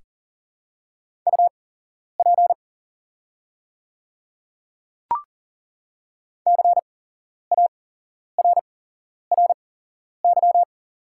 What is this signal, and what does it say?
Morse code, heard as a keyed tone, about 38 wpm, spaced as for 11 wpm, 690 Hz. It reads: UP CARRY
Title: 8x Character Spacing - Top 500 Words - 40wpm